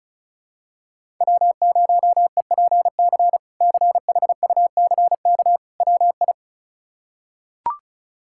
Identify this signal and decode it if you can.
Morse code, heard as a keyed tone, about 36 wpm, 700 Hz. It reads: W0EPC CHUCK WI
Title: K1USN SST QSO Practice - 35wpm